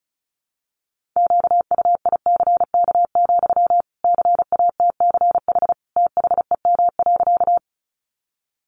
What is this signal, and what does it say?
Morse code, heard as a keyed tone, about 34 wpm, 700 Hz. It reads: QUICK, CATCH THEM.
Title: Sentences from Top 1500 Words 35wpm